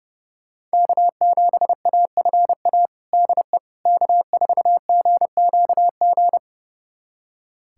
Morse code, heard as a keyed tone, about 30 wpm, 700 Hz. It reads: K7AFA DE K4GQG
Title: USA Callsign DE Callsign - 5 Characters - 30wpm